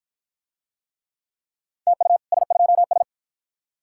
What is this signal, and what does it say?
Morse code, heard as a keyed tone, about 52 wpm, 700 Hz. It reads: TU S2S